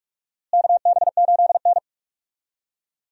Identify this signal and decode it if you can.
Morse code, heard as a keyed tone, about 46 wpm, 700 Hz. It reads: KB8N
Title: USA Call Signs - 4 Characters - Repeated 3x - 45wpm